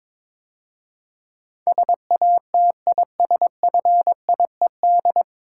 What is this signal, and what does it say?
Morse code, heard as a keyed tone, about 22 wpm, 700 Hz. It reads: SATISFIED
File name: 9 Letter Words - First 3 Letters Given - 22wpm